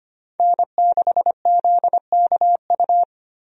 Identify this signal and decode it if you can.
Morse code, heard as a keyed tone, about 25 wpm, 700 Hz. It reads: N6ZKU
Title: USA Call Signs - 5 Characters - Repeated 2x - 25wpm